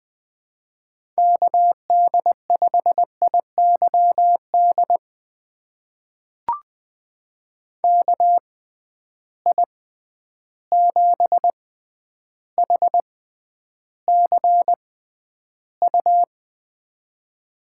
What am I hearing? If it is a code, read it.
Morse code, heard as a keyed tone, about 20 wpm, spaced as for 10 wpm, 700 Hz. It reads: KD5IYD KI7HCU
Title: USA Call Signs - 6 Characters - 6x Character Spacing - 20wpm